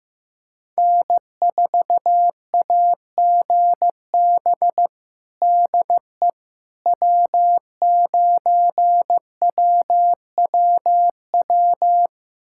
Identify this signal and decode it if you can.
Morse code, heard as a keyed tone, about 15 wpm, 700 Hz. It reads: N4AGB DE W9WWW
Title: USA Callsign DE Callsign - 5 Characters - 15wpm